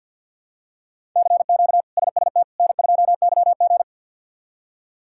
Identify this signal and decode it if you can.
Morse code, heard as a keyed tone, about 48 wpm, 700 Hz. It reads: CQ SST N2YG
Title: K1USN SST QSO Practice - 50wpm